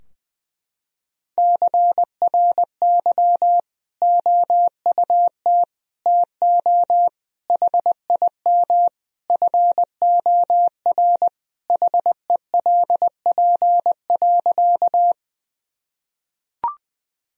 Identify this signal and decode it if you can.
Morse code, heard as a keyed tone, about 20 wpm, 700 Hz. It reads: CRY OUT TO HIM FOR HELP.
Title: Sentences from Top 1000 Words - Review Part 2 - 20wpm